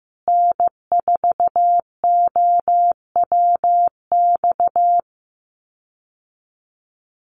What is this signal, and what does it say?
Morse code, heard as a keyed tone, about 15 wpm, 700 Hz. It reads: N4OWX
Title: USA Call Signs Repeated 3x 15wpm